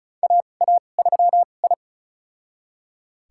Morse code, heard as a keyed tone, about 34 wpm, spaced as for 25 wpm, 700 Hz. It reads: AA3I
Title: USA Call Signs - 4 Characters - 2x Character Spacing - 35wpm